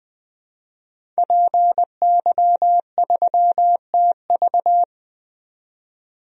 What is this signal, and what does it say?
Morse code, heard as a keyed tone, about 20 wpm, 700 Hz. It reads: PY3TV
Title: DXCC-100 Call Signs - 5 Characters - 20wpm